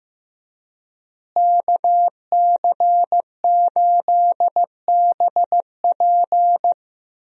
Morse code, heard as a keyed tone, about 16 wpm, 700 Hz. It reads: KC8BP
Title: USA Call Signs - 5 Characters - Repeated 2x - 15wpm